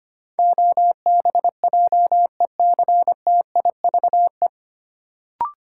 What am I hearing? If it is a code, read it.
Morse code, heard as a keyed tone, about 25 wpm, 700 Hz. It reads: OBJECTIVE
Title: Sentences from Top 1200 Words - New Word Drill - 25wpm